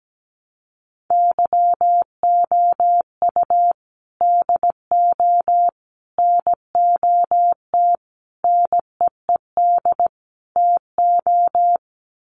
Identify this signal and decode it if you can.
Morse code, heard as a keyed tone, about 17 wpm, 700 Hz. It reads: YOU DO NOT NEED TO
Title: Sets of 5 Words 17wpm